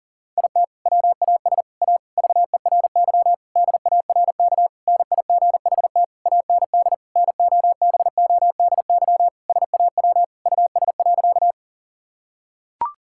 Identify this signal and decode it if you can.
Morse code, heard as a keyed tone, about 37 wpm, 700 Hz. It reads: IT WAS A VERY DARK NIGHT AND NOBODY SAW US.
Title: Sentences from Top 1600 Words 40wpm